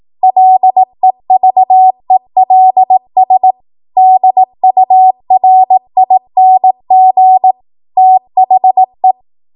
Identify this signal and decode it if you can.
Morse code, heard as a keyed tone, about 18 wpm, 750 Hz. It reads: LEVELS DURING THE
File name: Morse / 18WPM